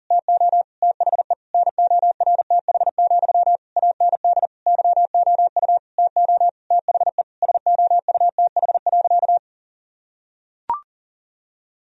Morse code, heard as a keyed tone, about 37 wpm, 700 Hz. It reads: TO THE NORTH, AND YOU TO THE SOUTH.
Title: Sentences from Top 1000 Words - Review Part 2 - 40wpm